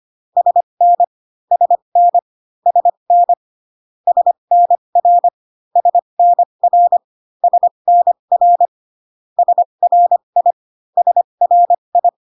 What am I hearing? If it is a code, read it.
Morse code, heard as a keyed tone, about 25 wpm, spaced as for 22 wpm, 700 Hz. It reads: SN SN SN SNR SNR SNR SRI SRI
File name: CW common abbreviations alphabetical repeated 3 times 25 18wpm